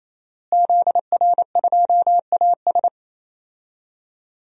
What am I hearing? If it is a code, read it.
Morse code, heard as a keyed tone, about 27 wpm, 700 Hz. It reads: ZR2AS
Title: DXCC-100 Call Signs - 5 Characters - Repeated 3x - 28wpm